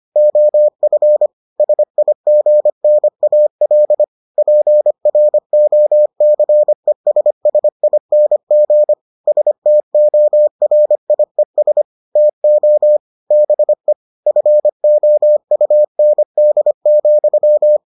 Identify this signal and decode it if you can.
Morse code, heard as a keyed tone, about 25 wpm, 600 Hz. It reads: OF SIGNAL PROCESSING STORIES TO BE FOUND,